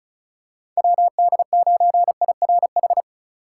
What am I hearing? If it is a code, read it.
Morse code, heard as a keyed tone, about 36 wpm, 700 Hz. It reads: WD9IRH
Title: USA Call Signs - 3x Character Spacing - 35wpm